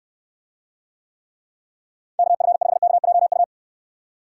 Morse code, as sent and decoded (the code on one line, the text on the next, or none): -.- ..-. ..... -.- .--- ..-
KF5KJU